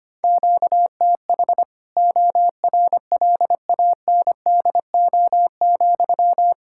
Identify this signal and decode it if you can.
Morse code, heard as a keyed tone, about 25 wpm, 700 Hz. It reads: QTH ORLANDO,
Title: Practice QSO - Exchange 1 - Beginning - 25wpm